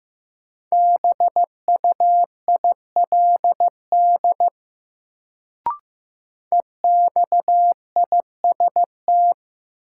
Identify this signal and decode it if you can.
Morse code, heard as a keyed tone, about 15 wpm, 700 Hz. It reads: BUILD EXIST
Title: Sentences from Top 1100 Words - New Word Drill Alt - 15wpm